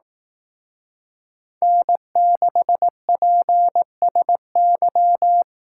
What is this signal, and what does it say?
Morse code, heard as a keyed tone, about 18 wpm, 700 Hz. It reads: N6PSY
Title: Speed Racing - USA 1x3 Call Signs 15wpm